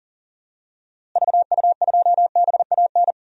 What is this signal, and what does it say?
Morse code, heard as a keyed tone, about 40 wpm, 700 Hz. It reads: VU2BAN
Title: DXCC-100 Call Signs - 6 Characters - Repeated 3x - 40wpm